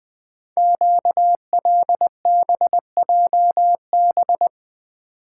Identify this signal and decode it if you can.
Morse code, heard as a keyed tone, about 20 wpm, 700 Hz. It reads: QLBJB
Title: Code Groups - 5 Letters - 20wpm